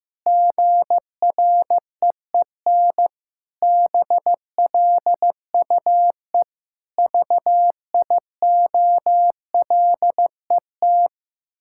Morse code, heard as a keyed tone, about 15 wpm, 700 Hz. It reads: GREEN BLUE VIOLET
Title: Pangram Challenge v2 - 15wpm